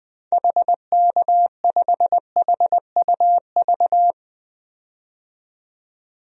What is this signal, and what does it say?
Morse code, heard as a keyed tone, about 21 wpm, 700 Hz. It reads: HK5HUV
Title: DXCC-100 Call Signs - 6 Characters - 20wpm